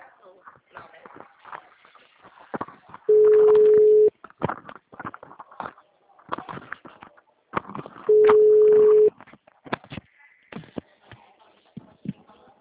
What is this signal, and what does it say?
Ringback tone (Europe, 425 Hz), with speech.